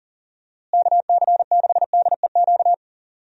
Morse code, heard as a keyed tone, about 40 wpm, 700 Hz. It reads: KC6DEQ
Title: USA Call Signs - 6 Characters - Repeated 2x - 40wpm